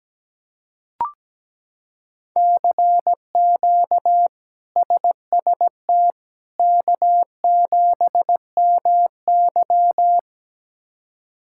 Morse code, heard as a keyed tone, about 17 wpm, 700 Hz. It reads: CQ SST K7MY
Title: K1USN SST QSO Practice - 17wpm